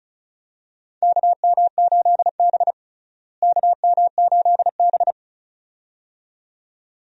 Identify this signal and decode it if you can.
Morse code, heard as a keyed tone, about 34 wpm, 700 Hz. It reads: KM8B KM8B
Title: USA Call Signs - 4 Characters - Repeated 2x - 35wpm